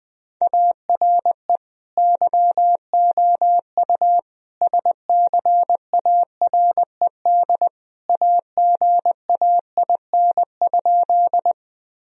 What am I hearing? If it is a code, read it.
Morse code, heard as a keyed tone, about 20 wpm, 700 Hz. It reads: ARE YOU SCARED AGAIN?
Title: Sentences from Top 1000 Words - Review Part 4 - 20wpm